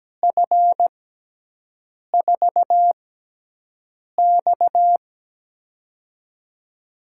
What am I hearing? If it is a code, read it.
Morse code, heard as a keyed tone, about 17 wpm, spaced as for 6 wpm, 700 Hz. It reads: F4X
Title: USA Call Signs - 4 Characters - 6x Character Spacing - 17wpm